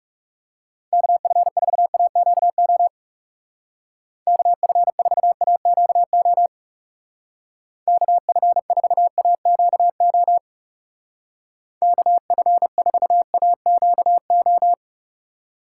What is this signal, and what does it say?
Morse code, heard as a keyed tone, about 35 wpm, 700 Hz. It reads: KF4AQO KF4AQO KF4AQO KF4AQO
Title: Speed Racing - USA 2x3 Call Signs 30wpm